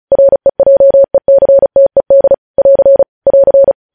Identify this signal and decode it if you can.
Morse code, heard as a keyed tone, about 36 wpm, 550 Hz. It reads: REJECTED <AR> <AR>
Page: news headlines in morse code at 35 wpm